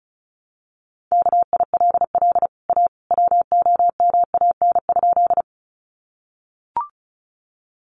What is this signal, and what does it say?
Morse code, heard as a keyed tone, about 34 wpm, 700 Hz. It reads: KILL A WOMAN?
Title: Sentences from Top 1400 Words 35wpm